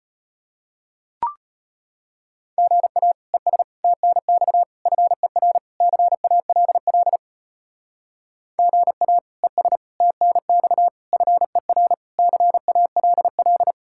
Morse code, heard as a keyed tone, about 31 wpm, 700 Hz. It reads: GA ES TNX FER CALL GA ES TNX FER CALL
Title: Speed-Racing Common QSO Phrases - 25wpm